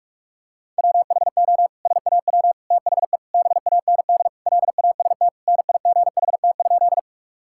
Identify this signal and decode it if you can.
Morse code, heard as a keyed tone, about 44 wpm, 700 Hz. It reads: WHO SAW THE BAND LAST NIGHT?
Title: Sentences from Top 1600 Words 45wpm